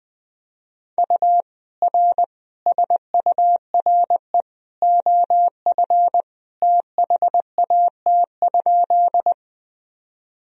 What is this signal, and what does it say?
Morse code, heard as a keyed tone, about 21 wpm, 700 Hz. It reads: U R SURE OF THAT?
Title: Sentences from Top 300 Words and Abbreviations 20wpm